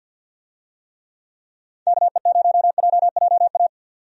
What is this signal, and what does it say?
Morse code, heard as a keyed tone, about 51 wpm, 700 Hz. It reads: KE0JJA
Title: USA Call Signs - 6 Characters - 50wpm